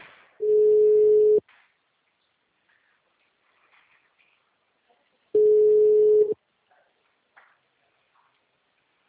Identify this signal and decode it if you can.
Ringback tone (Europe, 425 Hz).